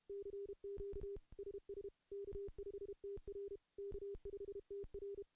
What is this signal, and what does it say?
Morse code, heard as a keyed tone, about 32 wpm, 400 Hz. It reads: CQ SS K5TR K5TR
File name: contests / CW-SS-k5tr- -0100